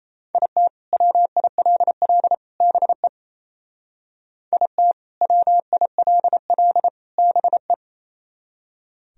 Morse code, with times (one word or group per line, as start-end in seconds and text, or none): 0.3-0.7 IT
0.9-2.3 WILL
2.6-3.1 BE
4.5-4.9 IT
5.2-6.9 WILL
7.2-7.7 BE